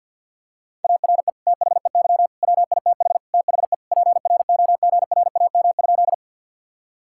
Morse code, heard as a keyed tone, about 50 wpm, 700 Hz. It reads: ARE THEY WITH THE PROGRAM?
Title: Sentences from Top 1000 Words - Review Part 5 - 50wpm